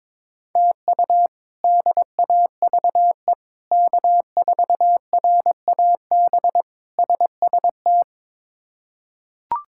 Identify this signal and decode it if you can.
Morse code, heard as a keyed tone, about 22 wpm, 700 Hz. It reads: TU DAVE K4RAB SST
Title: K1USN SST QSO Practice - 22wpm